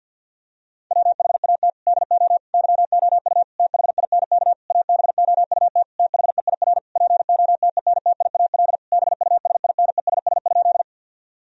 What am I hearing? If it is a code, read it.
Morse code, heard as a keyed tone, about 50 wpm, 700 Hz. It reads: WHAT DO YOU THINK ABOUT THEIR POTENTIAL BUSINESS?
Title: Sentences from Top 1000 Words - Review Part 4 - 50wpm